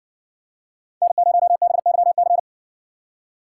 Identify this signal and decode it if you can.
Morse code, heard as a keyed tone, about 60 wpm, 700 Hz. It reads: N0BYC